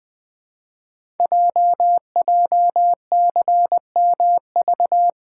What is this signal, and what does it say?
Morse code, heard as a keyed tone, about 20 wpm, 700 Hz. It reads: JJCMV
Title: Code Groups - 5 Letters - 20wpm